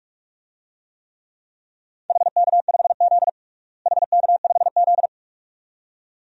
Morse code, heard as a keyed tone, about 42 wpm, 700 Hz. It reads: HK5Z HK5Z
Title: DXCC-100 Call Signs - 4 Characters - Repeated 2x - 45wpm